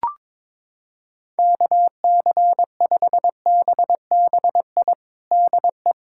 Morse code, heard as a keyed tone, about 22 wpm, 700 Hz. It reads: KC5BBI DE
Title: USA Callsign DE Callsign - 6 Characters - 22wpm